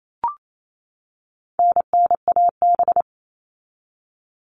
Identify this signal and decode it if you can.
Morse code, heard as a keyed tone, about 28 wpm, 700 Hz. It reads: NNAB
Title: All Cut Numbers - 4-Digit 28wpm